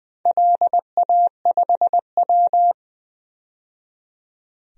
Morse code, heard as a keyed tone, about 20 wpm, 700 Hz. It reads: LA5W
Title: DXCC-100 Call Signs - 4 Characters - 20wpm